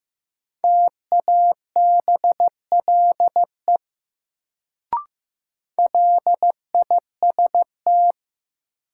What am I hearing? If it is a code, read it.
Morse code, heard as a keyed tone, about 15 wpm, 700 Hz. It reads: TABLE LIST
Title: Sentences from Top 400 Words - New Word Drill - 15wpm